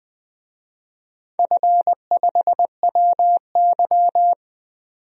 Morse code, heard as a keyed tone, about 20 wpm, 700 Hz. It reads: F5WY
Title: DXCC-100 Call Signs - 4 Characters - 20wpm